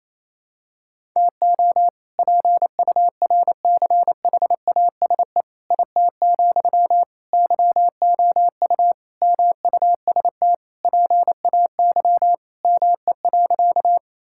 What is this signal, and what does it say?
Morse code, heard as a keyed tone, about 28 wpm, 700 Hz. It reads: TO PURCHASE IT, YOU MUST PAY ME.
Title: Sentences from Top 1000 Words - Review Part 5 - 28wpm